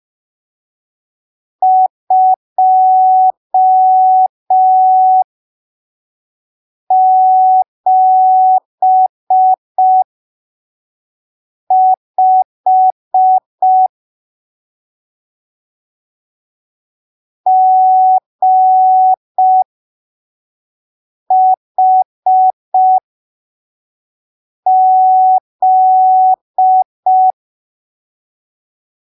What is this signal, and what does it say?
Morse code, heard as a keyed tone, about 5 wpm, spaced as for 3 wpm, 750 Hz. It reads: 275 GHZ
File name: arquivos / texto / texto 12 05ppm